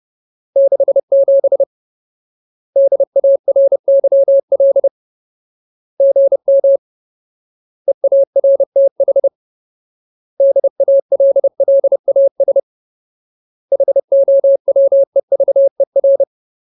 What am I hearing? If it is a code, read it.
Morse code, heard as a keyed tone, about 30 wpm, 550 Hz. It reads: 67 DARYL GM EARTH DALLAS HOWEVER
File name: Ultimate Word List / 30WPM Ultimate